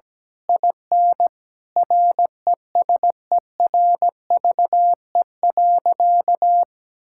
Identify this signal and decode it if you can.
Morse code, heard as a keyed tone, about 17 wpm, 700 Hz. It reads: IN RESERVE.
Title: Sentences from Top 1000 Words - Review Part 6 - 17wpm